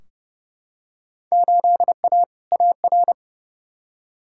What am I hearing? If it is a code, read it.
Morse code, heard as a keyed tone, about 32 wpm, 700 Hz. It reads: 8A AR
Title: Field Day Exchanges Encore - 30wpm